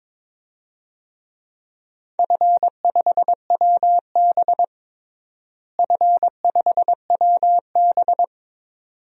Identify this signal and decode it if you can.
Morse code, heard as a keyed tone, about 22 wpm, 700 Hz. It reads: F5WB F5WB
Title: DXCC-100 Call Signs - 4 Characters - Repeated 2x - 22wpm